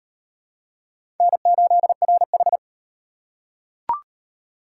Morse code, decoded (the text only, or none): N8RH